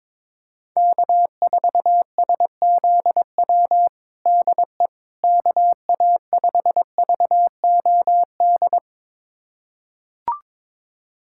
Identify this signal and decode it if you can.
Morse code, heard as a keyed tone, about 22 wpm, 700 Hz. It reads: K4SZW DE KA5VOD
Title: USA Callsign DE Callsign 22wpm